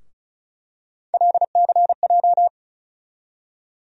Morse code, heard as a keyed tone, about 36 wpm, 700 Hz. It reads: LCJ
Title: USA Call Sign Postfixes - 3 Letters 35wpm